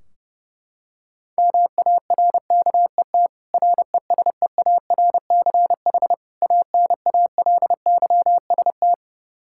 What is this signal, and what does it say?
Morse code, heard as a keyed tone, about 30 wpm, 700 Hz. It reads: MARKET RESEARCH ANALYST
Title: Practice QSO - Exchange 3 - More Info - 30wpm